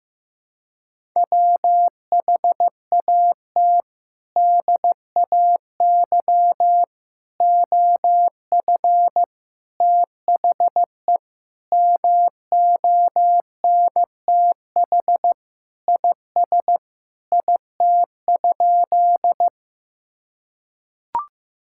Morse code, heard as a keyed tone, about 15 wpm, 700 Hz. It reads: WHAT DAY OF THE MONTH IS IT?
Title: Sentences from Top 1000 Words - Review Part 5 - 15wpm